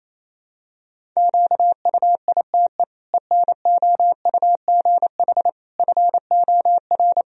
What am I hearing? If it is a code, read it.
Morse code, heard as a keyed tone, about 28 wpm, 700 Hz. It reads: QUITE ENOUGH FOR
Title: Sentences from Top 1000 Words - Review Part 1 - 28wpm